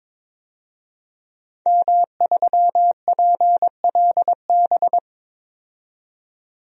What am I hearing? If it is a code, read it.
Morse code, heard as a keyed tone, about 22 wpm, 700 Hz. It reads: M3PLB
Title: DXCC-100 Call Signs - 5 Characters - 22wpm